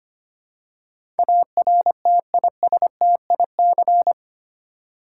Morse code, heard as a keyed tone, about 25 wpm, 700 Hz. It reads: ARTISTIC
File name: 8 Letter Words - First 3 Letters Given - 25wpm